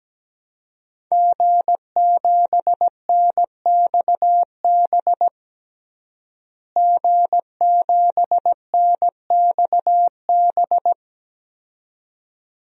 Morse code, heard as a keyed tone, about 17 wpm, 700 Hz. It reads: G7NXB G7NXB
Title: DXCC-100 Call Signs - 5 Characters - Repeated 2x - 17wpm